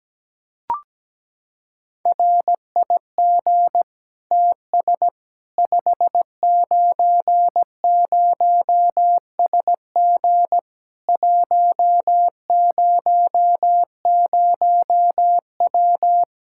Morse code, heard as a keyed tone, about 17 wpm, 700 Hz. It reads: RIG TS 590SG 100W
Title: K1USN SST QSO Practice 17wpm